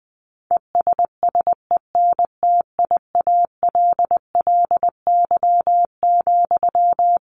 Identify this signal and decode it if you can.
Morse code, heard as a keyed tone, about 20 wpm, 700 Hz. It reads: ESSENTIALLY,
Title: Sentences from Top 1300 Words 20wpm